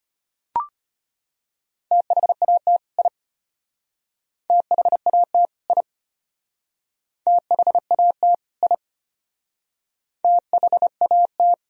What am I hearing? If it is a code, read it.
Morse code, heard as a keyed tone, about 32 wpm, 700 Hz. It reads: THAT I THAT I THAT I THAT